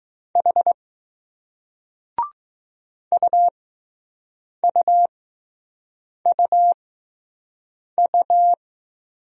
Morse code, heard as a keyed tone, about 20 wpm, spaced as for 6 wpm, 700 Hz. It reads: H UUUU